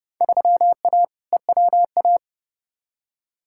Morse code, heard as a keyed tone, about 32 wpm, 700 Hz. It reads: 3A EWA
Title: Field Day Exchanges - 30wpm